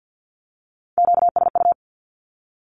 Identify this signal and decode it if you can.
Morse code, heard as a keyed tone, about 50 wpm, 700 Hz. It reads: QSU
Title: Q-Codes 50wpm